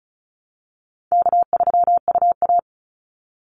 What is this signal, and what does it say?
Morse code, heard as a keyed tone, about 34 wpm, 700 Hz. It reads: K3UA
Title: Call Signs Repeated 2x - 35wpm